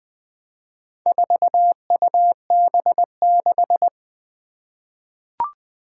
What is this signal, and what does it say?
Morse code, heard as a keyed tone, about 20 wpm, 700 Hz. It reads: 4UB6
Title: Four Challenging Letter-Number Combinations 20wpm